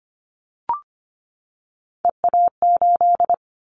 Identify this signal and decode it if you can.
Morse code, heard as a keyed tone, about 25 wpm, 700 Hz. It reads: EA8
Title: Callsign Prefixes for Top 100 DXCC - 25wpm